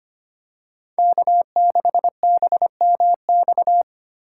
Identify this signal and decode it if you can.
Morse code, heard as a keyed tone, about 25 wpm, 700 Hz. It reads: K6BMX
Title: USA Call Signs - 4x Character Spacing - 25wpm